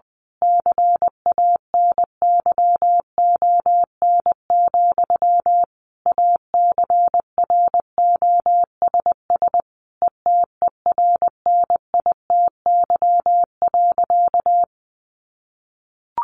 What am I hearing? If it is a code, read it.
Morse code, heard as a keyed tone, about 20 wpm, 700 Hz. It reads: CANYON, ACROSS ETERNITY.
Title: A Sound Of Thunder 20wpm